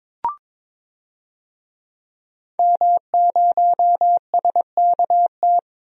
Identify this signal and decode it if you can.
Morse code, heard as a keyed tone, about 22 wpm, 700 Hz. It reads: M0SKT
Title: DXCC-100 Call Signs - 5 Characters - 22wpm